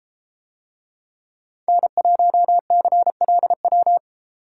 Morse code, heard as a keyed tone, about 32 wpm, 700 Hz. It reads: N1CLW